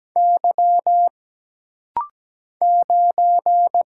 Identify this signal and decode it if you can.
Morse code, heard as a keyed tone, about 17 wpm, 700 Hz. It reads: Y 9
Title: Single Letter-Number Alt - Rapid-Fire 17wpm